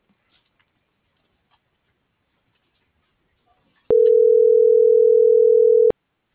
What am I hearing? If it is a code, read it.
Ringback tone (North America), 440 + 480 Hz.